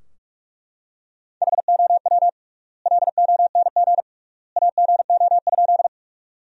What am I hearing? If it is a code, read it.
Morse code, heard as a keyed tone, about 42 wpm, 700 Hz. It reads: HOW LONG AGO?
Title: 3x Word-Spacing - Sentences from Top 500 Words - 45wpm